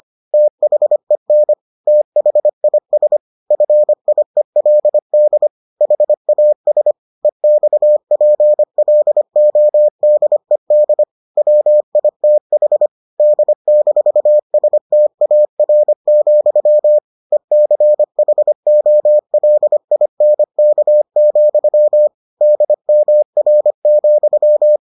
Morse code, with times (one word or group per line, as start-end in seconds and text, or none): 0.3-1.5 THEN
1.9-3.2 THIS
3.5-5.5 FIELD
5.8-6.9 HAS
7.2-11.0 EXPLODED
11.4-12.9 WITH
13.2-17.0 D-STAR,
17.3-22.1 ECHOLINK,
22.4-24.8 DMR,